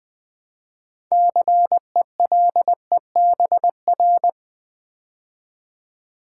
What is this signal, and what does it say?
Morse code, heard as a keyed tone, about 20 wpm, 700 Hz. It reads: CELEBR
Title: Build A Word - 3 Letters At A Time - 9-Letter Words - 20wpm